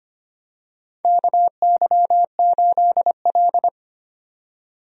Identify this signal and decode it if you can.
Morse code, heard as a keyed tone, about 25 wpm, 700 Hz. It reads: KY8L
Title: USA Call Signs - 4 Characters - 25wpm